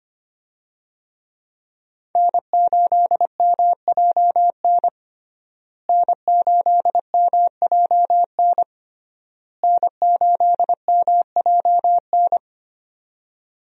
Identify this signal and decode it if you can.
Morse code, heard as a keyed tone, about 25 wpm, 700 Hz. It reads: N8MJN N8MJN N8MJN
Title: USA Call Signs - 5 Characters - Repeated 3x - 25wpm